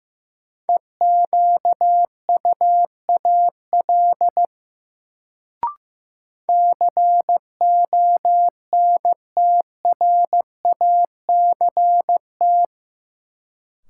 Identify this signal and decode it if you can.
Morse code, heard as a keyed tone, about 15 wpm, 700 Hz. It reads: EQUAL CONTRACT
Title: Sentences from Top 1400 Words - New Word Drill Alt - 15wpm